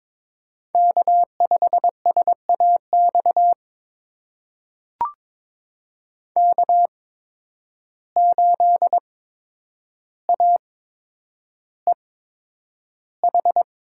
Morse code, heard as a keyed tone, about 22 wpm, spaced as for 9 wpm, 700 Hz. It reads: K5SAX K8AEH
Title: USA Call Signs - 5 Characters - 8x Character Spacing - 22wpm